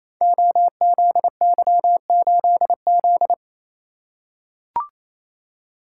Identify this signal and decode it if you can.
Morse code, heard as a keyed tone, about 28 wpm, 700 Hz. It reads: OZY8Z
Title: Code Groups - 5 Letter:Numbers - 28wpm